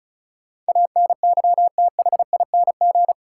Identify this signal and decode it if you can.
Morse code, heard as a keyed tone, about 36 wpm, 700 Hz. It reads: ANYTHING
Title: Sentences from Top 1000 Words - New Word Drill - 35wpm